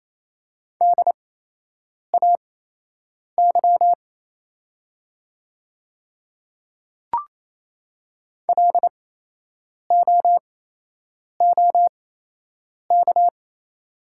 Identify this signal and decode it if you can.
Morse code, heard as a keyed tone, about 28 wpm, spaced as for 8 wpm, 700 Hz. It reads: DAY LOOK
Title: 8x Character Spacing - Top 300 Words - 28wpm